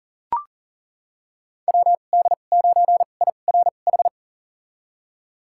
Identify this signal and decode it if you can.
Morse code, heard as a keyed tone, about 40 wpm, spaced as for 29 wpm, 700 Hz. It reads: WD9IRH
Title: USA Call Signs - 2x Character Spacing - 40wpm